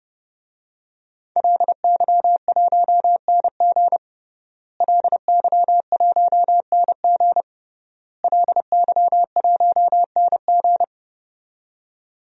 Morse code, heard as a keyed tone, about 30 wpm, 700 Hz. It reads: LY1NG LY1NG LY1NG
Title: DXCC-100 Call Signs - 5 Characters - Repeated 3x - 30wpm